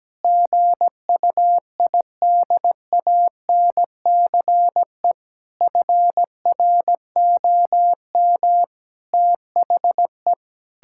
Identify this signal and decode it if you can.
Morse code, heard as a keyed tone, about 17 wpm, 700 Hz. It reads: GUIDANCE FROM THE
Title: Sentences from Top 1600 Words 17wpm